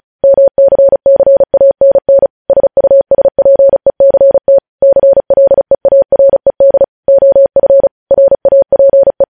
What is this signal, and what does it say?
Morse code, heard as a keyed tone, about 36 wpm, 550 Hz. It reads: MCCANN SUSPECT CLEARED OF RAPE